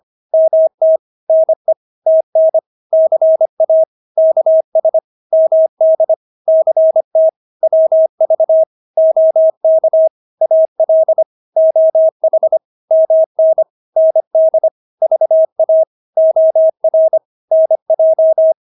Morse code, heard as a keyed tone, about 25 wpm, 640 Hz. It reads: MT NE TN CA KS MD CT WV OK AL OH MN ND VA OR NJ
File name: State-Abbreviations-25